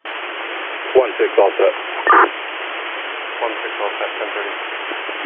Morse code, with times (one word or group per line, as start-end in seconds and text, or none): none